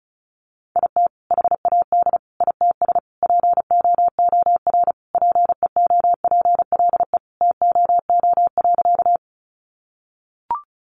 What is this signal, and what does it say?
Morse code, heard as a keyed tone, about 34 wpm, 700 Hz. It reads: IT HAD ITS POOR PEOPLE TOO.
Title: Sentences from Top 1000 Words 35wpm